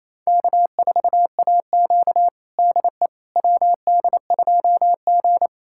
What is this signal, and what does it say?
Morse code, heard as a keyed tone, about 28 wpm, 700 Hz. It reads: K4AQ DE WD2G
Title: USA Callsign DE Callsign - 4 Characters - 28wpm